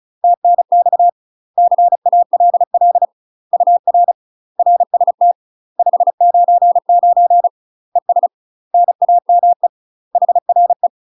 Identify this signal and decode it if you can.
Morse code, heard as a keyed tone, about 36 wpm, 700 Hz. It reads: TNX CALL UR RST 599 ES NAME HRE
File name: UK MASTER QSOs TRAINING / 35WPM / UK MASTER QSOs 0018